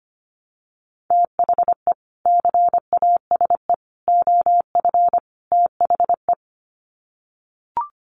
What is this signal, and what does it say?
Morse code, heard as a keyed tone, about 25 wpm, 700 Hz. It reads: THE CASE OF THE
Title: Sets of 4 Words 25wpm